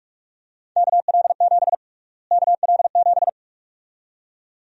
Morse code, heard as a keyed tone, about 46 wpm, 700 Hz. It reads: KL7 KL7
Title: USA Call Sign Prefixes - Repeated 2x - 45wpm